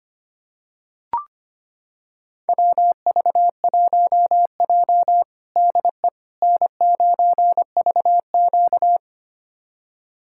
Morse code, heard as a keyed tone, about 25 wpm, 700 Hz. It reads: WV1J DE N9VQ
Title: USA Callsign DE Callsign - 4 Characters - 25wpm